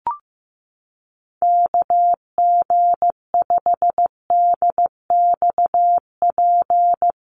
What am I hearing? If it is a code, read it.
Morse code, heard as a keyed tone, about 15 wpm, 700 Hz. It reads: KG5DXP
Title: USA Call Signs Repeated 3x 15wpm